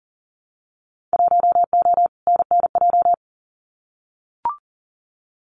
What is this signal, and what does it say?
Morse code, heard as a keyed tone, about 40 wpm, 700 Hz. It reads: 1O NNJ
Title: Winter Field Day Exchanges 40wpm